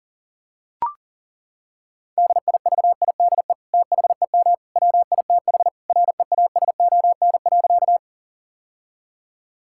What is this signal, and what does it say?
Morse code, heard as a keyed tone, about 40 wpm, 700 Hz. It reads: DIVIDE THEM WITH REASON.
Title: Sentences from Top 1600 Words 40wpm